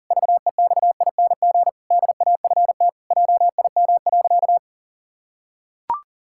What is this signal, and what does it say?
Morse code, heard as a keyed tone, about 37 wpm, 700 Hz. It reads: VEXING DAFT JIM.
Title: Pangram Challenge v2 - 40wpm